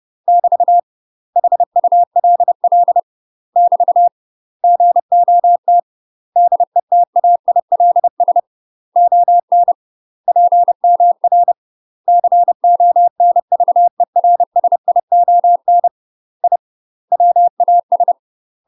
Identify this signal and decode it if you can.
Morse code, heard as a keyed tone, about 30 wpm, 700 Hz. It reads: <BT> HULL <BT> GOT DETAILS ON PMR CONVERSION I WAS
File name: UK MASTER QSOs TRAINING / 30WPM / UK MASTER QSOs 0049